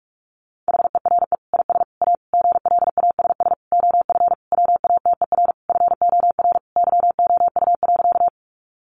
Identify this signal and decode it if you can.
Morse code, heard as a keyed tone, about 44 wpm, 700 Hz. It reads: HERE IS A GLASS OF WATER FOR YOU.
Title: Sentences from Top 1300 Words 45wpm